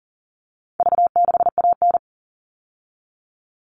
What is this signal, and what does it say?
Morse code, heard as a keyed tone, about 40 wpm, 700 Hz. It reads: V6AN
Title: All Cut Numbers - 4-Digit 40wpm